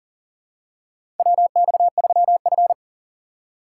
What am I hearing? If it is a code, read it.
Morse code, heard as a keyed tone, about 39 wpm, 700 Hz. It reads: WX3F
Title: USA Call Signs - 4x Character Spacing - 40wpm